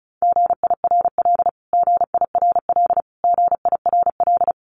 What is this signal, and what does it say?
Morse code, heard as a keyed tone, about 34 wpm, 700 Hz. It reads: GIRL GIRL GIRL
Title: Top 500 Words Repeated 35wpm